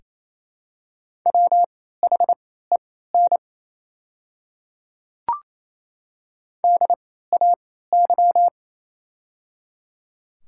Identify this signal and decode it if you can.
Morse code, heard as a keyed tone, about 28 wpm, spaced as for 16 wpm, 700 Hz. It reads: WHEN DAY
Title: 3x Character Spacing - Top 100 Words - 28wpm